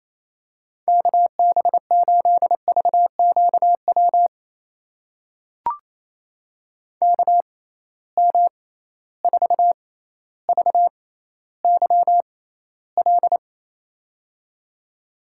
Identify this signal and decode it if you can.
Morse code, heard as a keyed tone, about 28 wpm, spaced as for 14 wpm, 700 Hz. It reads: KB8VQW KM4VYL
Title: USA Call Signs - 6 Characters - 6x Character Spacing - 28wpm